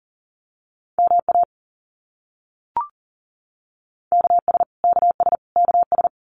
Morse code, heard as a keyed tone, about 40 wpm, 700 Hz. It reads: MA KS KS KS
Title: US States Repeated 3x 40wpm